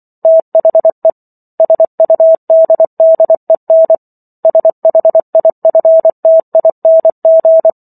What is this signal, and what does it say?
Morse code, heard as a keyed tone, about 24 wpm, 650 Hz. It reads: THE SUDDEN SHIFTING